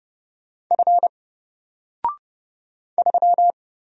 Morse code, heard as a keyed tone, about 30 wpm, 700 Hz. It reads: F 3
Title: Single Letter-Number 30wpm